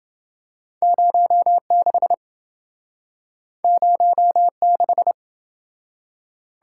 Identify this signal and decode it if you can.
Morse code, heard as a keyed tone, about 27 wpm, 700 Hz. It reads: 06 06